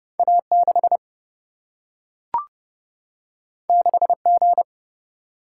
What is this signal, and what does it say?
Morse code, heard as a keyed tone, about 30 wpm, 700 Hz. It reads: A6 6G
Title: Single-Letter Single-Number 30wpm